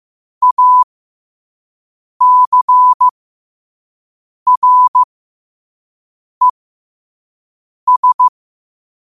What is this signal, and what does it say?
Morse code, heard as a keyed tone, about 15 wpm, spaced as for 5 wpm, 1000 Hz. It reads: ACRES
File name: qotdfiles / 5wpm